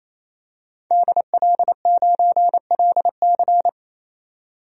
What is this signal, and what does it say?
Morse code, heard as a keyed tone, about 28 wpm, 700 Hz. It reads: DL9LC
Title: DXCC-100 Call Signs - 5 Characters - 28wpm